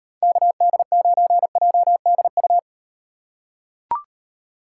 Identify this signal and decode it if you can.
Morse code, heard as a keyed tone, about 36 wpm, 700 Hz. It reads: KD9JDU